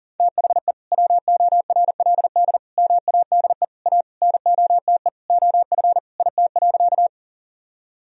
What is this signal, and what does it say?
Morse code, heard as a keyed tone, about 40 wpm, 700 Hz. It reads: THE WORLD MADE A NOTE OF IT.
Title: Sentences from Top 1000 Words - Review Part 2 - 40wpm